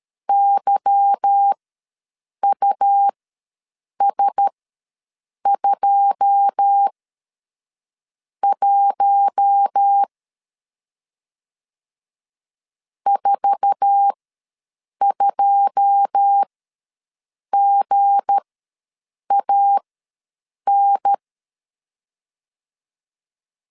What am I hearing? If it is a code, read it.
Morse code, heard as a keyed tone, about 13 wpm, spaced as for 6 wpm, 800 Hz. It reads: YUS21 42GAN
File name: LEZIONI / 9 2 ws 12wpm 3406